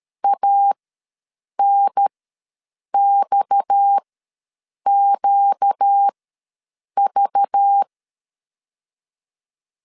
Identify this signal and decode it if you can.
Morse code, heard as a keyed tone, about 13 wpm, spaced as for 7 wpm, 800 Hz. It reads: ANXQV
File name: LEZIONI / 7 GF9Z / 7 GF ws 12wpm 3140